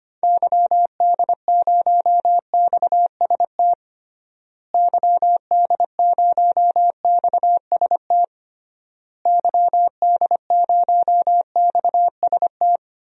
Morse code, heard as a keyed tone, about 25 wpm, 700 Hz. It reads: YD0XST YD0XST YD0XST
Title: DXCC-100 Call Signs - 6 Characters - Repeated 3x - 25wpm